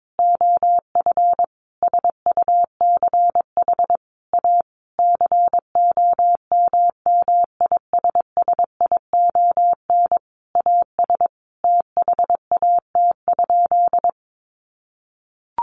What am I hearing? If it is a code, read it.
Morse code, heard as a keyed tone, about 22 wpm, 700 Hz. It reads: OF SUCH A COMMISSION AS THAT?
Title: Sentences from Top 1400 Words 22wpm